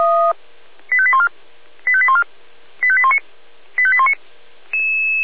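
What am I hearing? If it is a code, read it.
Five-tone selective call (ZVEI), heard as quick runs of five tones, tones of about 70 ms. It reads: 85814 85814 85819 85819
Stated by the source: ZVEI five-tone selective call received on an SDR.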